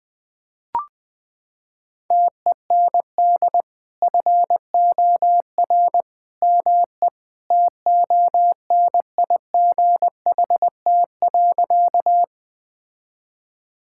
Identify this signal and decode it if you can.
Morse code, heard as a keyed tone, about 20 wpm, 700 Hz. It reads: TEND FOR ME TONIGHT.
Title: Sentences from Top 1000 Words - Review Part 5 - 20wpm